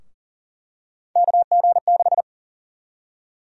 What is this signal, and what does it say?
Morse code, heard as a keyed tone, about 40 wpm, 700 Hz. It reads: KG6
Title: USA Call Sign Prefixes - 40wpm